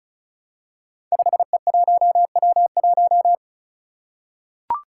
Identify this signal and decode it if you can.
Morse code, heard as a keyed tone, about 36 wpm, 700 Hz. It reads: 5E1W1
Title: Five Challenging Letter-Number Combinations 35wpm